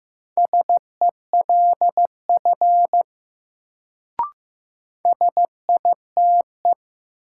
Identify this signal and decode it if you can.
Morse code, heard as a keyed tone, about 15 wpm, 700 Hz. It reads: SELF SITE
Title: Sentences from Top 1000 Words - Review Part 1 - New Word Drill - 15wpm